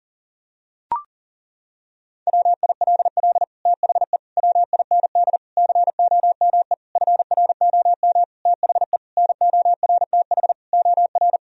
Morse code, heard as a keyed tone, about 40 wpm, 700 Hz. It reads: WILL THE WIND COME FROM THE NORTH OR
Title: Sentences from Top 1000 Words - Review Part 2 - 40wpm